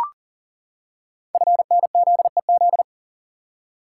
Morse code, heard as a keyed tone, about 42 wpm, 690 Hz. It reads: FNZEZ
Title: Code Groups - 5 Letters - 40wpm